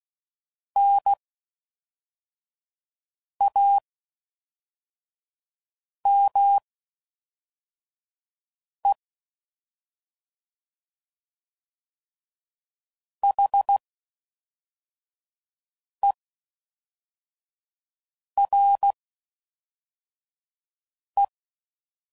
Morse code, heard as a keyed tone, about 16 wpm, spaced as for 4 wpm, 800 Hz. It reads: NAME HERE